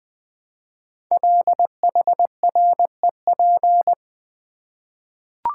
Morse code, heard as a keyed tone, about 20 wpm, 700 Hz. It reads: LHREP
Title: Code Groups - 5 Letters - 20wpm